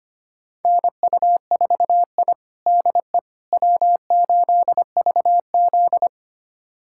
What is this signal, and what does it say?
Morse code, heard as a keyed tone, about 25 wpm, 700 Hz. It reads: NU4I DE W8VZ
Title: USA Callsign DE Callsign - 4 Characters - 25wpm